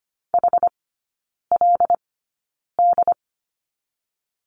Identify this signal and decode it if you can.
Morse code, heard as a keyed tone, about 25 wpm, spaced as for 9 wpm, 690 Hz. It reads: HLD
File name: Lesson 14e - Number 2 - Three Characters 25wpm